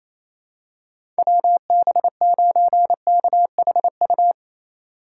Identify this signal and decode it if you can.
Morse code, heard as a keyed tone, about 28 wpm, 700 Hz. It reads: WB9KHU
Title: USA Call Signs - 2x Character Spacing - 28wpm